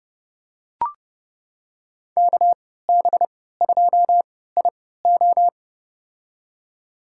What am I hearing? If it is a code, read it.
Morse code, heard as a keyed tone, about 30 wpm, spaced as for 17 wpm, 700 Hz. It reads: KB2IO
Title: USA Call Signs - 5 Characters - 3x Character Spacing - 30wpm